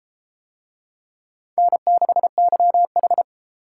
Morse code, heard as a keyed tone, about 32 wpm, 700 Hz. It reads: N6YH